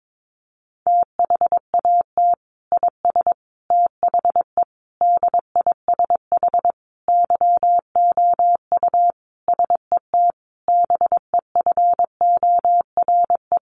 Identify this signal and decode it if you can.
Morse code, heard as a keyed tone, about 22 wpm, 700 Hz. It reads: THAT IS THE DISH YOU SET BEFORE
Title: Sentences from Top 1300 Words 22wpm